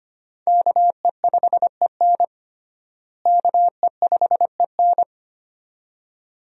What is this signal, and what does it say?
Morse code, heard as a keyed tone, about 25 wpm, 700 Hz. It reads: KE5EN KE5EN
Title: USA Call Signs - 5 Characters - Repeated 2x - 25wpm